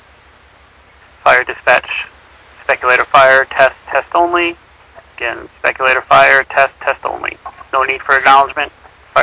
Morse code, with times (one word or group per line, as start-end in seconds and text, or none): none